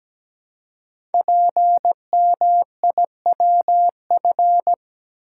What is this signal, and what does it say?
Morse code, heard as a keyed tone, about 17 wpm, 700 Hz. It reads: PMIWF
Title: Code Groups - 5 Letter:Numbers - 17wpm